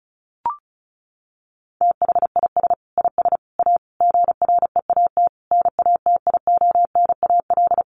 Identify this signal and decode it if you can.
Morse code, heard as a keyed tone, about 34 wpm, 700 Hz. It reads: THIS IS A GREAT NATIONAL
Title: Sentences from Top 1300 Words 35wpm